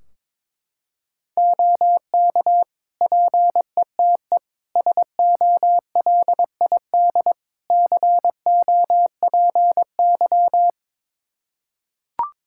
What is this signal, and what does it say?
Morse code, heard as a keyed tone, about 22 wpm, 700 Hz. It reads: OK PETE SOLID COPY
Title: Practice QSO - Exchange 3 - More Info - 22wpm